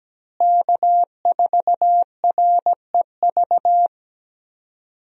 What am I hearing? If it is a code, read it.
Morse code, heard as a keyed tone, about 17 wpm, 700 Hz. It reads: K4REV
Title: USA Call Signs - 5 Characters - Repeated 2x - 17wpm